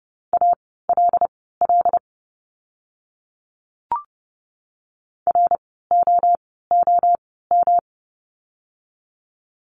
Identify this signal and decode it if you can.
Morse code, heard as a keyed tone, about 30 wpm, spaced as for 17 wpm, 700 Hz. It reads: ALL ROOM
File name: course / 30:10wpm / 108 Lesson 18f - Letter m - Words 30wpm